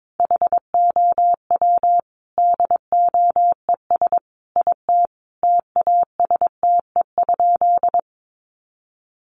Morse code, heard as a keyed tone, about 22 wpm, 700 Hz. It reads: HOW DOES IT TASTE?
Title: Sentences from Top 1100 Words 22wpm